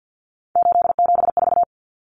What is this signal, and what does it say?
Morse code, heard as a keyed tone, about 50 wpm, 700 Hz. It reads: 874
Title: Numbers - 3-Digit 50wpm